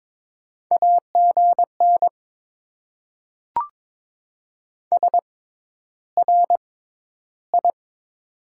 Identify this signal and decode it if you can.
Morse code, heard as a keyed tone, about 22 wpm, spaced as for 11 wpm, 700 Hz. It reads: AGN SRI
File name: Top 90 QSO Elements - 6x Character Spacing - 22wpm